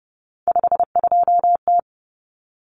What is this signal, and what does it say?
Morse code, heard as a keyed tone, about 30 wpm, 700 Hz. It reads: 52T
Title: Common Cut Numbers - 3-Digit 30wpm